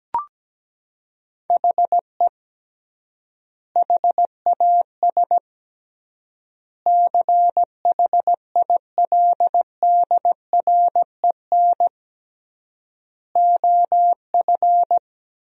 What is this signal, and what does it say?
Morse code, heard as a keyed tone, about 17 wpm, 700 Hz. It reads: HE HAS CHILDREN OF
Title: 3x Word-Spacing - Sentences from Top 300 Words - 17wpm